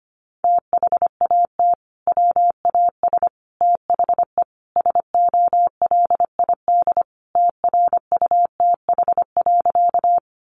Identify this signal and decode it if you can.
Morse code, heard as a keyed tone, about 25 wpm, 700 Hz. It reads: THAT WAS THE SOLID TRUTH.
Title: Sentences from Top 1500 Words 25wpm